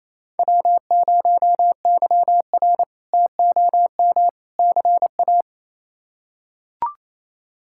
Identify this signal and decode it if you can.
Morse code, heard as a keyed tone, about 28 wpm, 700 Hz. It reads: W0YR TOM CA
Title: K1USN SST QSO Practice - 28wpm